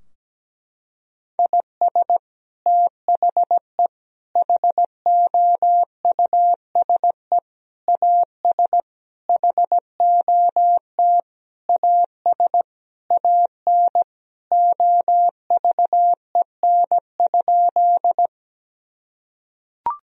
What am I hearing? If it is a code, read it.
Morse code, heard as a keyed tone, about 17 wpm, 700 Hz. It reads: IS THE HOUSE AS HOT AS AN OVEN?
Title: Sentences from Top 1000 Words - Review Part 4 - 17wpm